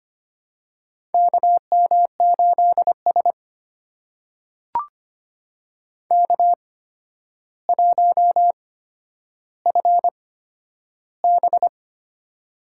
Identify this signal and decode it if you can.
Morse code, heard as a keyed tone, about 25 wpm, spaced as for 11 wpm, 700 Hz. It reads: KM8S K1FB
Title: USA Call Signs - 4 Characters - 8x Character Spacing - 25wpm